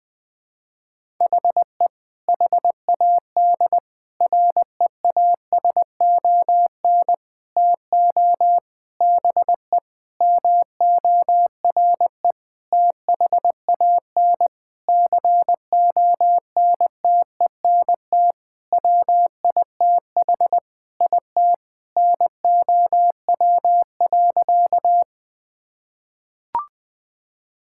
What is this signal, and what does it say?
Morse code, heard as a keyed tone, about 21 wpm, 700 Hz. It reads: HE HAD REASON TO BE MORE THAN CONTENT WITH IT NOW.
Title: Sentences from Top 1000 Words - Review Part 6 - 20wpm